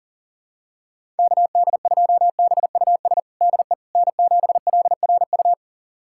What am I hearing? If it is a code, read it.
Morse code, heard as a keyed tone, about 37 wpm, 700 Hz. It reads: KD2BUS DE N7LRU
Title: USA Callsign DE Callsign 40wpm